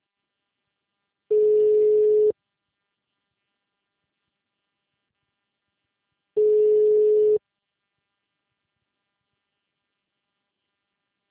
Ringback tone (Europe, 425 Hz).